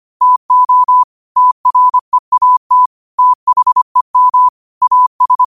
Morse code, heard as a keyed tone, about 25 wpm, 1000 Hz. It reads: TO TREAT THEM AS